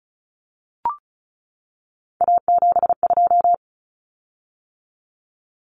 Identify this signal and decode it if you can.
Morse code, heard as a keyed tone, about 34 wpm, 700 Hz. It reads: A72
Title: Common Cut Numbers - 3-Digit 35wpm